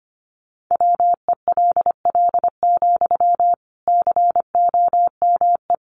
Morse code, heard as a keyed tone, about 25 wpm, 700 Hz. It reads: WELL, COME
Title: Sentences from Top 1300 Words 25wpm